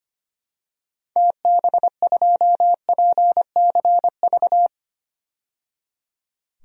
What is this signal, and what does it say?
Morse code, heard as a keyed tone, about 25 wpm, 700 Hz. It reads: TB2PCV
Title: DXCC-100 Call Signs - 6 Characters - 25wpm